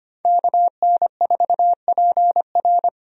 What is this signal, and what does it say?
Morse code, heard as a keyed tone, about 25 wpm, 700 Hz. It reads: KN4PR
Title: USA Call Signs - 5 Characters - Repeated 2x - 25wpm